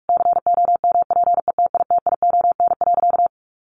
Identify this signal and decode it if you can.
Morse code, heard as a keyed tone, about 44 wpm, 700 Hz. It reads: COMPETITION.
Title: Additional Easy Phrases 45wpm